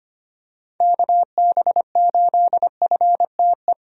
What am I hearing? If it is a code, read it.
Morse code, heard as a keyed tone, about 25 wpm, 700 Hz. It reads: KB8FTE
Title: USA Call Signs - 6 Characters - 25wpm